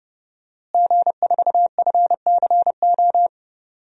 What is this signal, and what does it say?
Morse code, heard as a keyed tone, about 30 wpm, 700 Hz. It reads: G4FCO
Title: DXCC-100 Call Signs - 5 Characters - Repeated 3x - 30wpm